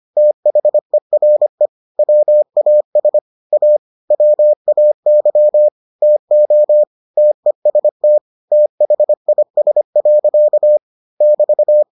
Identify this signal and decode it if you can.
Morse code, heard as a keyed tone, about 25 wpm, 600 Hz. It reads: THERE WAS A WAY TO TEST THIS. <BT>